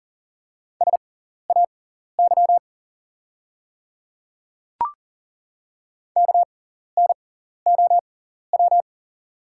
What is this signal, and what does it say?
Morse code, heard as a keyed tone, about 40 wpm, spaced as for 14 wpm, 700 Hz. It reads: SAY KNOW
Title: 6x Character Spacing - Top 200 Words - 40wpm